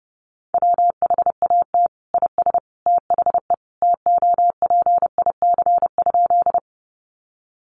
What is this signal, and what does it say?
Morse code, heard as a keyed tone, about 30 wpm, 700 Hz. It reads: WHAT IS THE TOPIC?
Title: Sentences from Top 1200 Words 30wpm